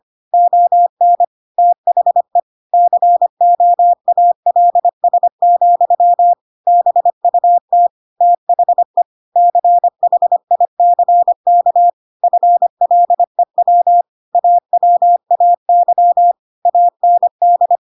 Morse code, heard as a keyed tone, about 25 wpm, 700 Hz. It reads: ON THE COALS, BUT THE CHICK FLEW AWAY AND
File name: Pinocchio / Pinocchio ch7 25 25 ik1hge